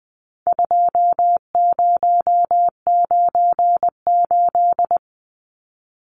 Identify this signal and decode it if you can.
Morse code, heard as a keyed tone, about 20 wpm, 700 Hz. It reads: 2098
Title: Numbers - 4-Digit 20wpm